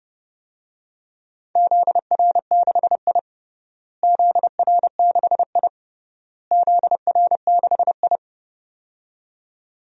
Morse code, heard as a keyed tone, about 31 wpm, 700 Hz. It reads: ZR6I ZR6I ZR6I
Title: DXCC-100 Call Signs - 4 Characters - Repeated 3x - 30wpm